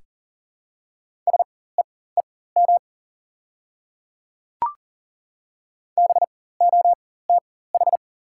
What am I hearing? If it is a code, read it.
Morse code, heard as a keyed tone, about 40 wpm, spaced as for 19 wpm, 700 Hz. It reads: SEEM BOTH
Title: 4x Character Spacing - Top 300 Words - 40wpm